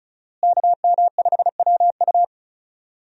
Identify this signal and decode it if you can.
Morse code, heard as a keyed tone, about 34 wpm, 700 Hz. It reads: KM5WU
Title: USA Call Signs - 5 Characters - Repeated 2x - 35wpm